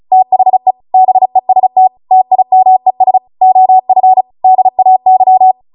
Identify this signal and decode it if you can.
Morse code, heard as a keyed tone, about 36 wpm, 750 Hz. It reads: THE BEST TIMES OF DAY